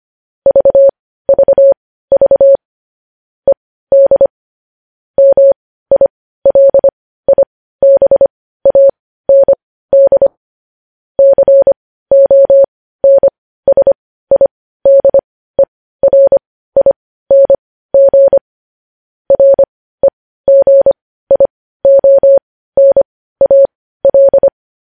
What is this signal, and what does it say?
Morse code, heard as a keyed tone, about 25 wpm, spaced as for 15 wpm, 550 Hz. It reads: VVV ED MILIBAND CONSIDERING REGIONAL
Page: news headlines in morse code at 15 wpm